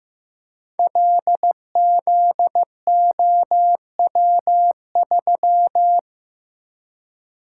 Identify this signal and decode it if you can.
Morse code, heard as a keyed tone, about 15 wpm, 700 Hz. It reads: LZOW3
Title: Code Groups - 5 Letter:Numbers - 15wpm